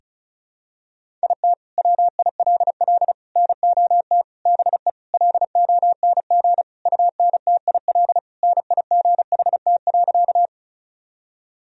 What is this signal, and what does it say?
Morse code, heard as a keyed tone, about 35 wpm, 700 Hz. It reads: IT WILL NOT BE LONG UNTIL NIGHT.
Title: Sentences from Top 1000 Words - Review Part 1 - 35wpm